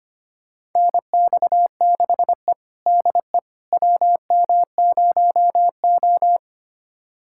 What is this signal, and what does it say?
Morse code, heard as a keyed tone, about 25 wpm, 700 Hz. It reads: NX6E DE WM0O
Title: USA Callsign DE Callsign - 4 Characters - 25wpm